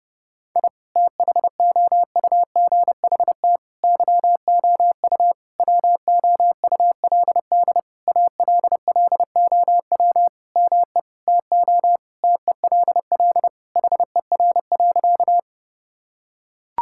Morse code, heard as a keyed tone, about 30 wpm, 700 Hz. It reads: I THOUGHT YOU WOULD ALLOW ME TO TELL HER.
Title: Sentences from Top 1000 Words - Review Part 5 - 30wpm